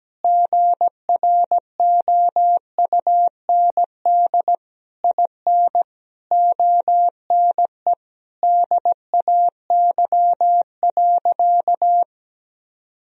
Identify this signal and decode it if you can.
Morse code, heard as a keyed tone, about 17 wpm, 700 Hz. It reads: GROUND IN ONE DAY.
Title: Sentences from Top 1000 Words - Review Part 3 - 17wpm